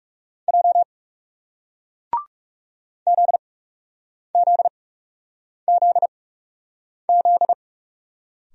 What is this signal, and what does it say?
Morse code, heard as a keyed tone, about 40 wpm, spaced as for 8 wpm, 700 Hz. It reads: J ZZZZ